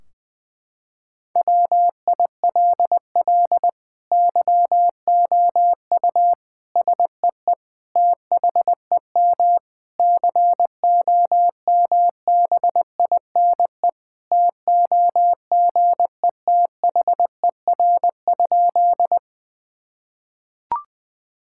Morse code, heard as a keyed tone, about 20 wpm, 700 Hz. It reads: WILL YOU SEE THEM COMBINE TOGETHER?
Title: Sentences from Top 1600 Words 20wpm